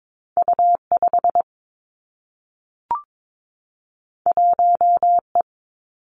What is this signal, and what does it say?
Morse code, heard as a keyed tone, about 22 wpm, 700 Hz. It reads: U 5 1 E
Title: Two Challenging Letter-Number Combinations 22wpm